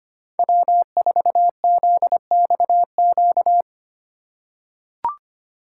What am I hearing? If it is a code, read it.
Morse code, heard as a keyed tone, about 25 wpm, 700 Hz. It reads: W4ZXQ
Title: USA Call Signs - 5 Characters - 2x Character Spacing - 25wpm